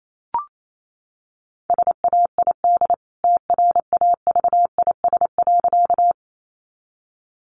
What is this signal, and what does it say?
Morse code, heard as a keyed tone, about 28 wpm, 700 Hz. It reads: SAID TRAVIS.
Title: A Sound Of Thunder 28wpm